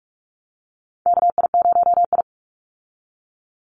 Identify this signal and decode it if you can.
Morse code, heard as a keyed tone, about 44 wpm, 700 Hz. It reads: KI0I
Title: Call Signs Repeated 2x - 45wpm